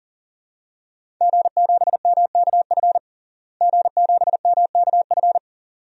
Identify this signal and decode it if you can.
Morse code, heard as a keyed tone, about 40 wpm, 700 Hz. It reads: G7MKF G7MKF
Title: DXCC-100 Call Signs - 5 Characters - Repeated 2x - 40wpm